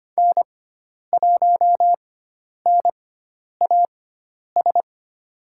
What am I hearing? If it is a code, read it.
Morse code, heard as a keyed tone, about 25 wpm, spaced as for 10 wpm, 700 Hz. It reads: N1NAS
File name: Lesson 16g - Letter c - Call Signs 25wpm